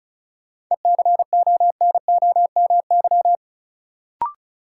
Morse code, heard as a keyed tone, about 34 wpm, 700 Hz. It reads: ECONOMY
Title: Sentences from Top 1200 Words - New Word Drill Alt - 35wpm